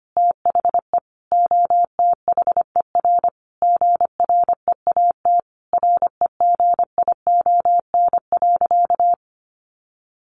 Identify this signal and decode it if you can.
Morse code, heard as a keyed tone, about 25 wpm, 700 Hz. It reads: THE OTHER GREAT REGION.
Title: Sentences from Top 1500 Words 25wpm